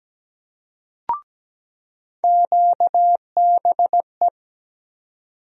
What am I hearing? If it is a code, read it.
Morse code, heard as a keyed tone, about 17 wpm, 700 Hz. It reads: QBE
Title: USA Call Sign Postfixes - 3 Letters 17wpm